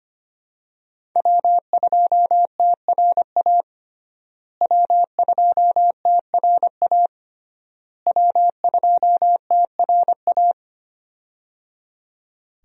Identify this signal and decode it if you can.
Morse code, heard as a keyed tone, about 25 wpm, 700 Hz. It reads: W2TRA W2TRA W2TRA
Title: USA Call Signs - 5 Characters - Repeated 3x - 25wpm